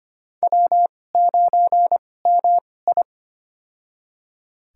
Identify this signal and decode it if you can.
Morse code, heard as a keyed tone, about 25 wpm, spaced as for 18 wpm, 700 Hz. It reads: W9MI
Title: USA Call Signs - 2x Character Spacing - 25wpm